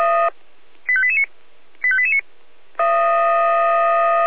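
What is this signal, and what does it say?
Five-tone selective call (ZVEI), heard as quick runs of five tones, tones of about 70 ms. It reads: 85889 85889
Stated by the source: ZVEI five-tone call 85356 received on SDR.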